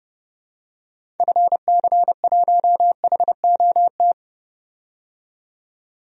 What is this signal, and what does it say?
Morse code, heard as a keyed tone, about 30 wpm, 700 Hz. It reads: FC1HOT
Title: DXCC-100 Call Signs - 6 Characters - 30wpm